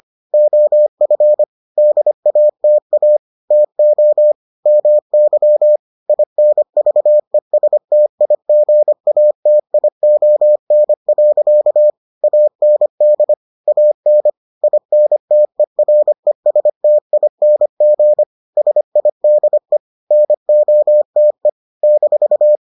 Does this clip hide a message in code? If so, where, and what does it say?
Yes: Morse code from 0.3 to 22.6 s, heard as a keyed tone, about 25 wpm, 600 Hz. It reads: OF DATA TO MY INVESTIGATION. AND AN INTERESTING SIDE NOTE -